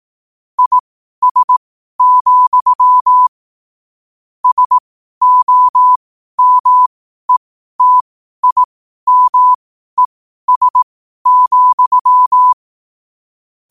Morse code, heard as a keyed tone, about 18 wpm, spaced as for 12 wpm, 1000 Hz. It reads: IS, SOMETIMES,
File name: qotdfiles / 12wpm